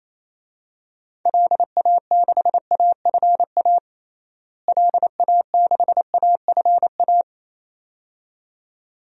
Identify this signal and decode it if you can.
Morse code, heard as a keyed tone, about 28 wpm, 700 Hz. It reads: LA6AFA LA6AFA
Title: DXCC-100 Call Signs - 6 Characters - Repeated 2x - 28wpm